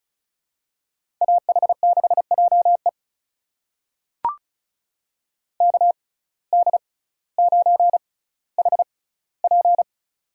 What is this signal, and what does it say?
Morse code, heard as a keyed tone, about 36 wpm, spaced as for 18 wpm, 700 Hz. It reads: AH6JE KD9HP
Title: USA Call Signs - 5 Characters - 6x Character Spacing - 35wpm